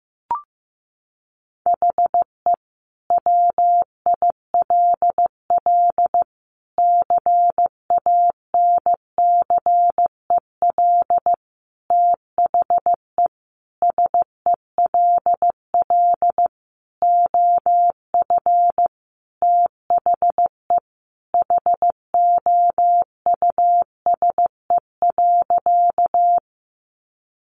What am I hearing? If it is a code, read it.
Morse code, heard as a keyed tone, about 15 wpm, 700 Hz. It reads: HE WILL CANCEL THE SELL OF THE HOUSE.
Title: Sentences from Top 1200 Words 15wpm